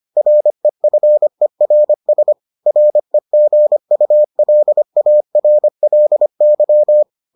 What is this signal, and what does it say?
Morse code, heard as a keyed tone, about 25 wpm, 600 Hz. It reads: REFERS REGULARLY